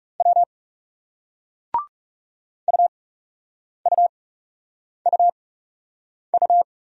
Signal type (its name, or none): Morse code